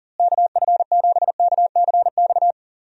Morse code, heard as a keyed tone, about 40 wpm, 700 Hz. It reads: KF7KCX
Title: USA Call Signs - 6 Characters - Repeated 3x - 40wpm